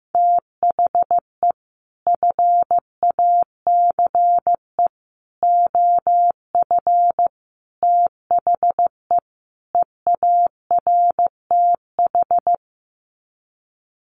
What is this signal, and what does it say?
Morse code, heard as a keyed tone, about 15 wpm, 700 Hz. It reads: THE FACE OF THE EARTH
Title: Sets of 5 Words Encore 15wpm